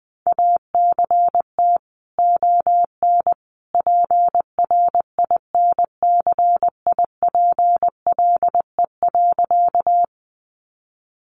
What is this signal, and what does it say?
Morse code, heard as a keyed tone, about 20 wpm, 700 Hz. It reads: ACT ON PRINCIPLE.
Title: Sentences from Top 1400 Words 20wpm